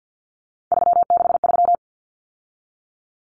Morse code, heard as a keyed tone, about 50 wpm, 700 Hz. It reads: 363
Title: Common Cut Numbers - 3-Digit 50wpm